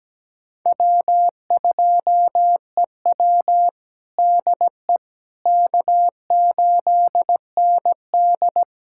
Morse code, heard as a keyed tone, about 17 wpm, 700 Hz. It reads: W2EW DE K8ND
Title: USA Callsign DE Callsign - 4 Characters - 17wpm